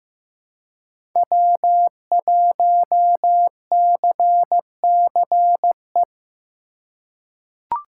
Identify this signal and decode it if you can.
Morse code, heard as a keyed tone, about 15 wpm, 700 Hz. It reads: W1CCE
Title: K1USN SST QSO Practice - 15wpm